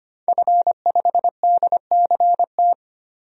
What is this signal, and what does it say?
Morse code, heard as a keyed tone, about 25 wpm, 700 Hz. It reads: F5DCT
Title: DXCC-100 Call Signs - 5 Characters - 25wpm